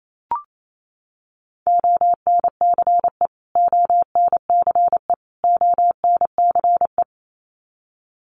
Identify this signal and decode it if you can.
Morse code, heard as a keyed tone, about 28 wpm, 700 Hz. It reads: ONCE ONCE ONCE
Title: Top 500 Words Repeated 28wpm